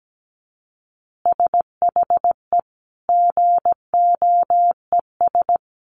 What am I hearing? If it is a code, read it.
Morse code, heard as a keyed tone, about 17 wpm, 700 Hz. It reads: SHE GOES
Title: Easy Phrases 17wpm